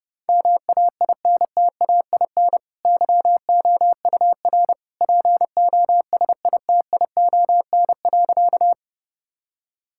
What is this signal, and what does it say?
Morse code, heard as a keyed tone, about 32 wpm, 700 Hz. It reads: MAINTAIN YOUR POSITION.
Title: Sentences from Top 1000 Words - Review Part 5 - 30wpm